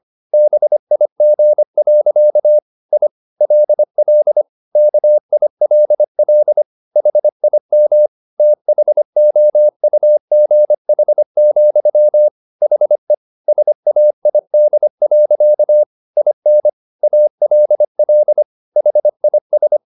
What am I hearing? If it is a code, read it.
Morse code, heard as a keyed tone, about 25 wpm, 600 Hz. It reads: BIG. I LL KILL HIM THOUGH, HE SAID. IN ALL HIS
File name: the old man and the sea part 1 25wpm